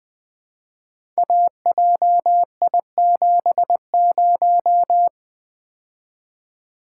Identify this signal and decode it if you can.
Morse code, heard as a keyed tone, about 20 wpm, 700 Hz. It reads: AJI70
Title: Code Groups - 5 Letter:Numbers - 20wpm